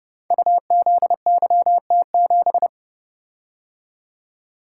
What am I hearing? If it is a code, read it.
Morse code, heard as a keyed tone, about 30 wpm, 700 Hz. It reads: UZYT7
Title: Code Groups - 5 Letter:Numbers - 30wpm